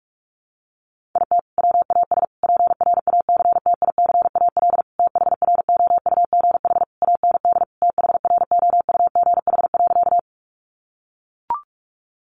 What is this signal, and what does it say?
Morse code, heard as a keyed tone, about 44 wpm, 700 Hz. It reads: IT WAS PRACTICAL THROUGH AND THROUGH.
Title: Sentences from Top 1400 Words 45wpm